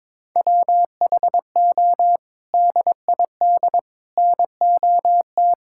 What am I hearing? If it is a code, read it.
Morse code, heard as a keyed tone, about 22 wpm, 700 Hz. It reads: WHO DID NOT
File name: practice / 22wpm / Sets of 7 Words - 22wpm